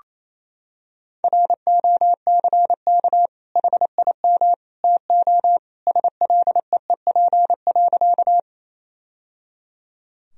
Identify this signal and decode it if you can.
Morse code, heard as a keyed tone, about 28 wpm, 700 Hz. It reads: ROCK HIM TO SLEEP.
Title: Sentences from Top 1000 Words - Review Part 2 - 28wpm